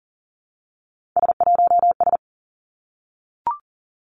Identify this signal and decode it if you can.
Morse code, heard as a keyed tone, about 40 wpm, 700 Hz. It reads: S1S
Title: Three Challenging Letter-Number Combinations 40wpm